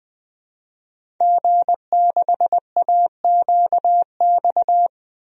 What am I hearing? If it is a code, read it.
Morse code, heard as a keyed tone, about 20 wpm, 700 Hz. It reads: G6AQX
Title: DXCC-100 Call Signs - 5 Characters - Repeated 3x - 20wpm